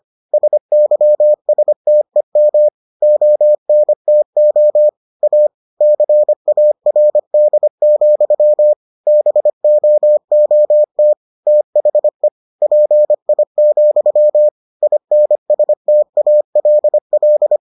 Morse code, heard as a keyed tone, about 25 wpm, 600 Hz. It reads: SYSTEM ONTO A CARD, BOOT THE PI, INSTALL